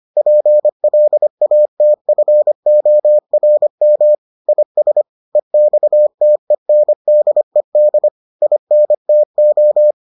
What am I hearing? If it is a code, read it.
Morse code, heard as a keyed tone, about 25 wpm, 600 Hz. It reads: PLATFORM IS EXTENDED INTO